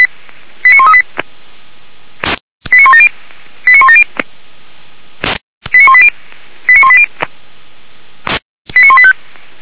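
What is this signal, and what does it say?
Five-tone selective call (ZVEI), heard as quick runs of five tones, tones of about 70 ms. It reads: 89138 89170 89170 89180 89180 89185
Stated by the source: Five-tone ZVEI call 89136, received by SDR; fire brigade Feuerwehr Orlamünde.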